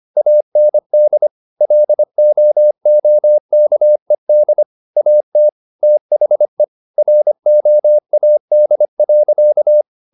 Morse code, heard as a keyed tone, about 25 wpm, 600 Hz. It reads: AND LOOKED AT THE ROAD.